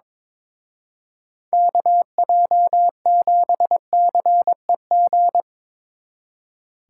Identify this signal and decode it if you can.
Morse code, heard as a keyed tone, about 25 wpm, 700 Hz. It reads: KJ7CEG
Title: Speed Racing - USA Call Signs 22wpm